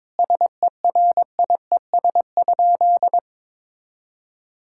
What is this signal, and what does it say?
Morse code, heard as a keyed tone, about 21 wpm, 700 Hz. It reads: SERIES?
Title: Sentences from Top 1000 Words - Review Part 5 - 22wpm